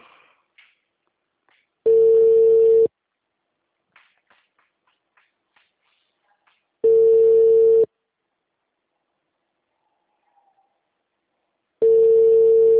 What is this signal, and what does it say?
Ringback tone (China, 450 Hz).